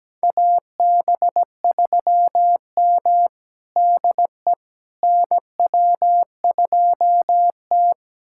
Morse code, heard as a keyed tone, about 17 wpm, 700 Hz. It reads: AB3M DE NW2T
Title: USA Callsign DE Callsign - 4 Characters - 17wpm